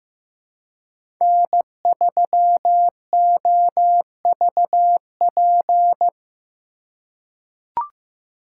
Morse code, heard as a keyed tone, about 15 wpm, 700 Hz. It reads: N3OVP
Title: USA Call Signs - 5 Characters - 2x Character Spacing - 15wpm